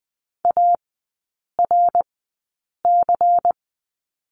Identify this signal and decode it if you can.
Morse code, heard as a keyed tone, about 20 wpm, spaced as for 8 wpm, 700 Hz. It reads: ARC
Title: 100 Lesson 17e - Letter u - Three Characters 20wpm